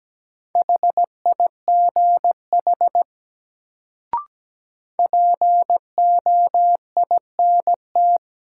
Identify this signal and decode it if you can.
Morse code, heard as a keyed tone, about 17 wpm, 700 Hz. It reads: HIGH POINT
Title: Sentences from Top 200 Words - New Word Drill - 17wpm